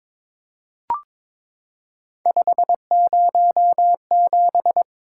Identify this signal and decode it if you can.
Morse code, heard as a keyed tone, about 23 wpm, 700 Hz. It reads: 507
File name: practice / 22wpm / Three Letter-Number Combinations - 22wpm